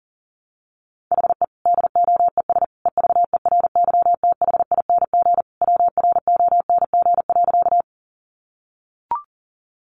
Morse code, heard as a keyed tone, about 40 wpm, 700 Hz. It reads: HE DOES EVERYTHING WRONG.
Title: Sentences from Top 900 Words 40wpm